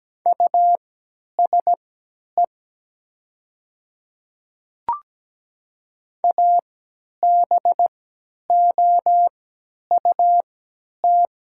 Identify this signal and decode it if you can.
Morse code, heard as a keyed tone, about 17 wpm, spaced as for 10 wpm, 700 Hz. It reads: USE ABOUT
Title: 3x Character Spacing - Top 100 Words - 17wpm